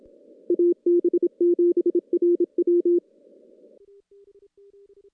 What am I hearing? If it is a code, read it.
Morse code, heard as a keyed tone, about 27 wpm, 350 Hz. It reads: AB7RW
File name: contests / CW-SS-k5tr- -0000